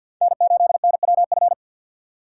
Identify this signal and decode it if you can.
Morse code, heard as a keyed tone, about 60 wpm, 700 Hz. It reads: N8NWF
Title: Speed Racing - USA 1x3 Call Signs 50wpm